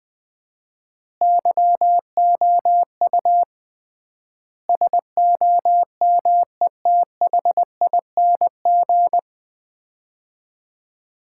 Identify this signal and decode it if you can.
Morse code, heard as a keyed tone, about 20 wpm, 700 Hz. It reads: YOU SOMETHING
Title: 3x Word-Spacing - Sets of 5 Words - 20wpm